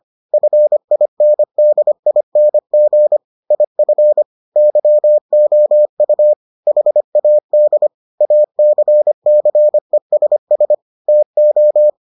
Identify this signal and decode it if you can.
Morse code, heard as a keyed tone, about 25 wpm, 600 Hz. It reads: FINDING IF YOU HAD ACCESS TO